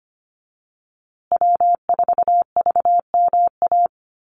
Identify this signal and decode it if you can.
Morse code, heard as a keyed tone, about 25 wpm, 700 Hz. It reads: W4VMA
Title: USA Call Signs Repeated 2x 25wpm